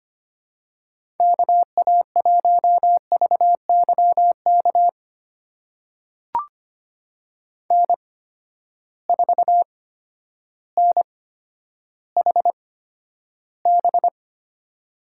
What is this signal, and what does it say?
Morse code, heard as a keyed tone, about 25 wpm, spaced as for 11 wpm, 700 Hz. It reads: KA1VYK N4NHB
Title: USA Call Signs - 8x Character Spacing - 25wpm